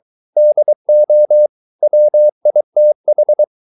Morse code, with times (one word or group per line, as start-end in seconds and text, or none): 0.4-1.5 DO
1.8-3.4 WITH